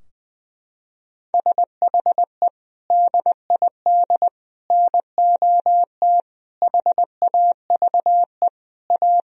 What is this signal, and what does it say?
Morse code, heard as a keyed tone, about 20 wpm, 700 Hz. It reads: SHE DID NOT HAVE A
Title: Sentences from Top 1000 Words - Review Part 2 - 20wpm